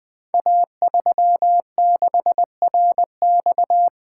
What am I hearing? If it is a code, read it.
Morse code, heard as a keyed tone, about 21 wpm, 700 Hz. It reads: A36RX
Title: Code Groups - 5 Letter:Numbers - 20wpm